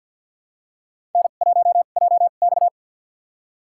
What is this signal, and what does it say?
Morse code, heard as a keyed tone, about 52 wpm, spaced as for 36 wpm, 700 Hz. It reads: N1JX
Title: USA Call Signs - 4 Characters - 2x Character Spacing - 50wpm